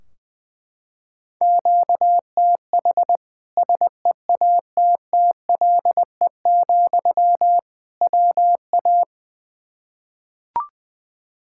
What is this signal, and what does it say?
Morse code, heard as a keyed tone, about 21 wpm, 700 Hz. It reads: QTH SEATTLE, WA
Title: Practice QSO - Exchange 1 - Beginning - 20wpm